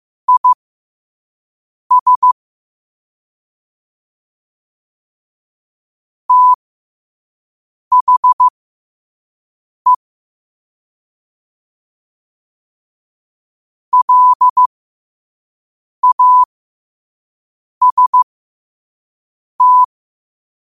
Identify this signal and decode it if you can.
Morse code, heard as a keyed tone, about 15 wpm, spaced as for 5 wpm, 1000 Hz. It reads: IS THE LAST
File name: qotdfiles / 5wpm